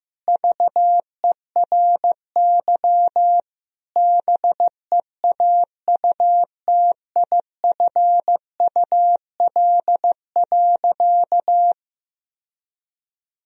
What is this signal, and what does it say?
Morse code, heard as a keyed tone, about 15 wpm, 700 Hz. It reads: VERY BEAUTIFUL.
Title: Sentences from Top 1000 Words - Review Part 1 - 15wpm